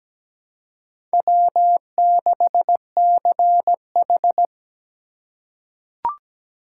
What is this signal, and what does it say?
Morse code, heard as a keyed tone, about 17 wpm, 700 Hz. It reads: W6CH
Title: USA Call Signs - 4x Character Spacing - 17wpm